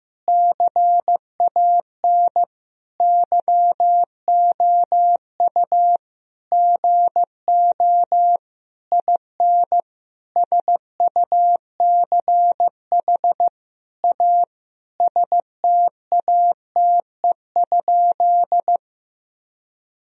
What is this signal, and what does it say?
Morse code, heard as a keyed tone, about 15 wpm, 700 Hz. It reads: CAN YOU GO IN SUCH A STATE?
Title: Sentences from Top 1000 Words - Review Part 2 - 15wpm